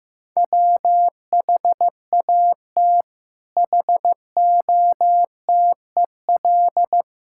Morse code, heard as a keyed tone, about 15 wpm, 700 Hz. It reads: WHAT HOTEL
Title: Sentences from Top 1600 Words 15wpm